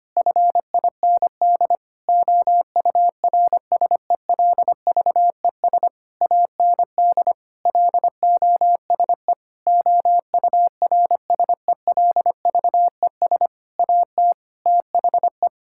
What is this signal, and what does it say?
Morse code, heard as a keyed tone, about 25 wpm, 700 Hz. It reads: FIND OURSELVES AND LOSE OURSELVES AT THE
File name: practice / 25wpm / Difficult Sentences with Hints - 25wpm